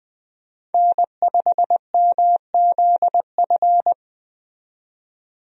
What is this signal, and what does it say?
Morse code, heard as a keyed tone, about 20 wpm, 700 Hz. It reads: N5MZF
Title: USA Call Signs - 5 Characters - 20wpm